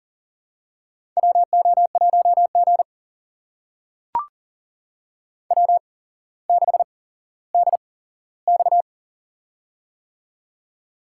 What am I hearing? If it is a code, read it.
Morse code, heard as a keyed tone, about 42 wpm, spaced as for 17 wpm, 700 Hz. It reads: WO1G W6DX
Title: USA Call Signs - 4 Characters - 8x Character Spacing - 40wpm